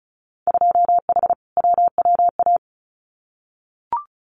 Morse code, heard as a keyed tone, about 34 wpm, 700 Hz. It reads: 2H WWA
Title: Winter Field Day Exchanges 35wpm